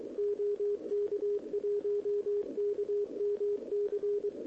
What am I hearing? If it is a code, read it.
Morse code, heard as a keyed tone, about 23 wpm, 410 Hz, with radio noise. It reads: OK1KMC
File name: OK QRP 2026 / OKQRP2026